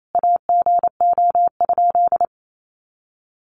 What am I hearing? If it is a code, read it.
Morse code, heard as a keyed tone, about 28 wpm, 700 Hz. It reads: AGO?
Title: Sentences from Top 1300 Words 28wpm